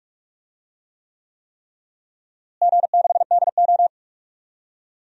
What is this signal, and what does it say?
Morse code, heard as a keyed tone, about 43 wpm, 700 Hz. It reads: G6DO
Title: DXCC-100 Call Signs - 4 Characters - Repeated 3x - 45wpm